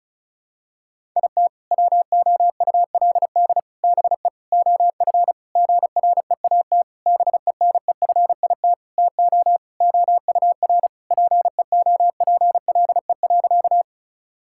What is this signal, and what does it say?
Morse code, heard as a keyed tone, about 34 wpm, 690 Hz. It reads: IT WOULD BE OF GREAT BENEFIT TO OUR PEOPLE.
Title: Sentences from Top 1000 Words - Review Part 6 - 35wpm